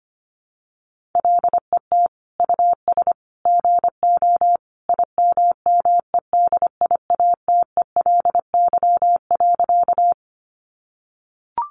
Morse code, heard as a keyed tone, about 25 wpm, 700 Hz. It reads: LET US GO IMMEDIATELY.
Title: Sentences from Top 1100 Words 25wpm